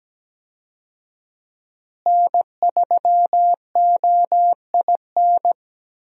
Morse code, heard as a keyed tone, about 17 wpm, 700 Hz. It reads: N3OIN
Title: USA Call Signs - 5 Characters - Repeated 3x - 17wpm